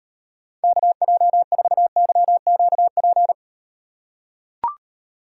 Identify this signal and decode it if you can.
Morse code, heard as a keyed tone, about 38 wpm, 700 Hz. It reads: KJ4YQP